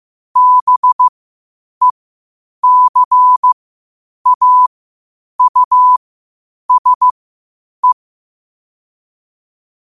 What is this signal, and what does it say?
Morse code, heard as a keyed tone, about 15 wpm, spaced as for 8 wpm, 1000 Hz. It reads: BECAUSE
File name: qotdfiles / 8wpm